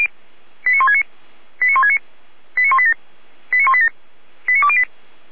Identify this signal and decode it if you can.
Five-tone selective call (ZVEI), heard as quick runs of five tones, tones of about 70 ms. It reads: 89169 89169 89187 89187 89208